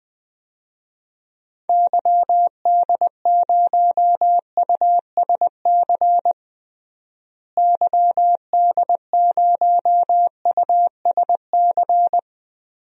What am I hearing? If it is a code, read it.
Morse code, heard as a keyed tone, about 20 wpm, 700 Hz. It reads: YD0USC YD0USC
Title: DXCC-100 Call Signs - 6 Characters - Repeated 2x - 20wpm